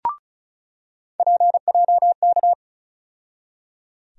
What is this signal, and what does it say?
Morse code, heard as a keyed tone, about 36 wpm, 700 Hz. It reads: PJK
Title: USA Call Sign Postfixes - 3 Letters 35wpm